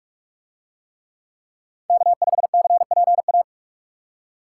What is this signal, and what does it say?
Morse code, heard as a keyed tone, about 40 wpm, 700 Hz. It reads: K5CPA